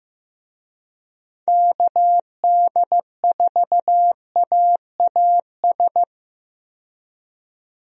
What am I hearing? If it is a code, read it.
Morse code, heard as a keyed tone, about 15 wpm, 700 Hz. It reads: KD4AAS
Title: USA Call Signs - 6 Characters - 15wpm